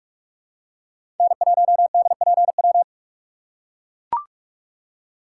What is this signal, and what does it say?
Morse code, heard as a keyed tone, about 44 wpm, 700 Hz. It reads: N1DPW
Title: USA Call Signs - 5 Characters - 6x Character Spacing - 45wpm